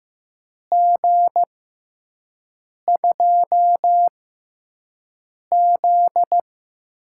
Morse code, heard as a keyed tone, about 15 wpm, spaced as for 5 wpm, 700 Hz. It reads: G2Z
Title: USA Call Signs - 4 Characters - 6x Character Spacing - 15wpm